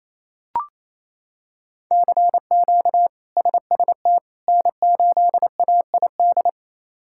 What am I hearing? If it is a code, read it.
Morse code, heard as a keyed tone, about 28 wpm, 700 Hz. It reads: CQ SST N8AID
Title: K1USN SST QSO Practice - 28wpm